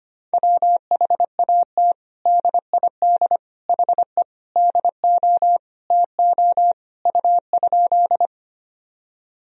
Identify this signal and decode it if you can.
Morse code, heard as a keyed tone, about 25 wpm, 700 Hz. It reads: WHAT DID HE DO TO U?
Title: Sentences from Top 100 Words and Abbreviations 25wpm